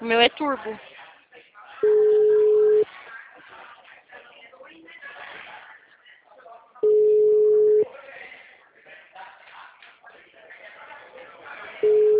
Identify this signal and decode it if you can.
Ringback tone (Europe, 425 Hz), with speech.
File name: out- -2011- - - .44648